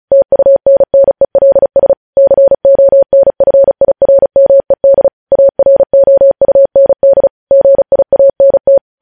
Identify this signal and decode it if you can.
Morse code, heard as a keyed tone, about 36 wpm, 550 Hz. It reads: TUNNELS CONFIRMED AROUND GIANT
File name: podcast / News headlines-1x-35x35